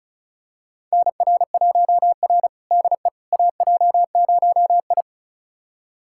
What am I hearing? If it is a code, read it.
Morse code, heard as a keyed tone, about 36 wpm, 700 Hz. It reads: NR1R DE AJ0I
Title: USA Callsign DE Callsign - 4 Characters - 35wpm